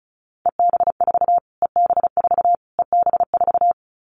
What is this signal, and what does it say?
Morse code, heard as a keyed tone, about 34 wpm, 700 Hz. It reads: EB4 EB4 EB4
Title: Callsign Prefixes for Top 25 DXCC - Repeated 3x - 35wpm